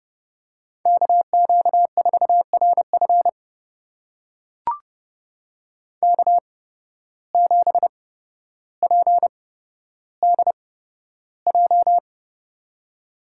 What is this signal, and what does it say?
Morse code, heard as a keyed tone, about 30 wpm, spaced as for 13 wpm, 700 Hz. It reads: KQ4RF K7PDJ
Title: USA Call Signs - 5 Characters - 8x Character Spacing - 30wpm